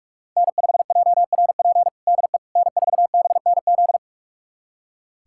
Morse code, heard as a keyed tone, about 41 wpm, 700 Hz. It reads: N5JRP DE N4BNZ
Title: USA Callsign DE Callsign - 5 Characters - 45wpm